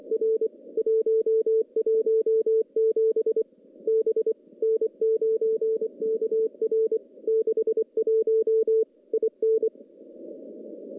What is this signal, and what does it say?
Morse code, heard as a keyed tone, about 24 wpm, 440 Hz. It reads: R 117 B N9KR 61 IN
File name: contests / CW-SS-k5tr- -0200